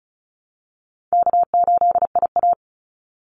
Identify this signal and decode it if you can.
Morse code, heard as a keyed tone, about 34 wpm, 700 Hz. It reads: K8IA
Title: Call Signs - 35wpm